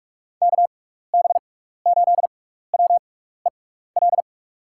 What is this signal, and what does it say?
Morse code, heard as a keyed tone, about 44 wpm, spaced as for 15 wpm, 700 Hz. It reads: KB8WEL
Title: USA Call Signs - 6 Characters - 6x Character Spacing - 45wpm